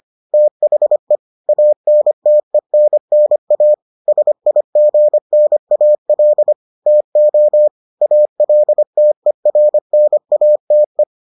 Morse code, heard as a keyed tone, about 25 wpm, 600 Hz. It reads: THE ANTENNA SIGNAL TO ALTERNATE